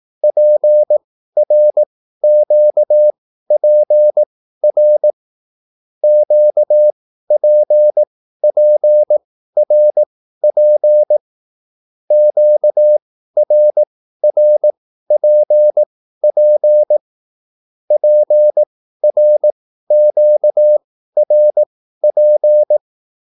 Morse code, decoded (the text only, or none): PRQPR QPPRP QRRPP PRQRP